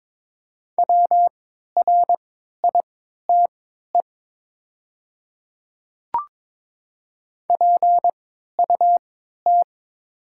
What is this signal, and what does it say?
Morse code, heard as a keyed tone, about 22 wpm, spaced as for 12 wpm, 700 Hz. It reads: WRITE PUT
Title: 3x Character Spacing - Top 200 Words - 22wpm